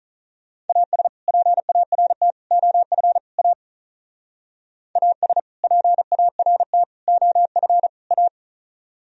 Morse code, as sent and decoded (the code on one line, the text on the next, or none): .- ... / .--. .- .-. - / --- ..-. / .- / .- ... / .--. .- .-. - / --- ..-. / .-
AS PART OF A AS PART OF A